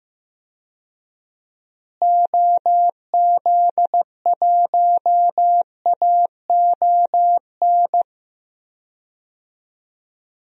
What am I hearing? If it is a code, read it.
Morse code, heard as a keyed tone, about 15 wpm, 700 Hz. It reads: OZ1AON
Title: DXCC-100 Call Signs - 6 Characters - Repeated 3x - 15wpm